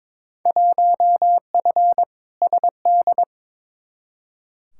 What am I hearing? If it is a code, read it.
Morse code, heard as a keyed tone, about 22 wpm, 700 Hz. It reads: 1F SD
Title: Field Day Exchanges - 22wpm